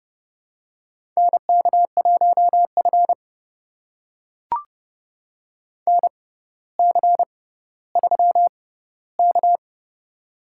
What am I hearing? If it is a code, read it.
Morse code, heard as a keyed tone, about 30 wpm, spaced as for 15 wpm, 700 Hz. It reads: NK1F NC3K
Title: USA Call Signs - 4 Characters - 6x Character Spacing - 30wpm